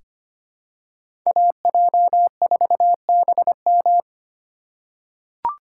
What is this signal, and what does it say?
Morse code, heard as a keyed tone, about 25 wpm, 700 Hz. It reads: AJ4BM
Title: USA Call Signs - 5 Characters - 4x Character Spacing - 25wpm